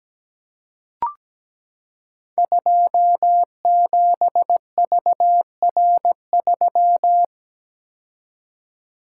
Morse code, heard as a keyed tone, about 17 wpm, 700 Hz. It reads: 27VR3
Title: Code Groups - 5 Letter:Numbers - 17wpm